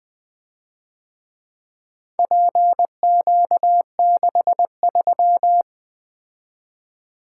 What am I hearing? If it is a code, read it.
Morse code, heard as a keyed tone, about 24 wpm, 700 Hz. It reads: PQ63
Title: Speed-Racing - Four Letter-Number Combinations - 20wpm